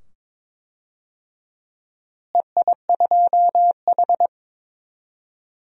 Morse code, heard as a keyed tone, about 22 wpm, 700 Hz. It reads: EI2H
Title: DXCC-100 Call Signs - 4 Characters - Repeated 2x - 22wpm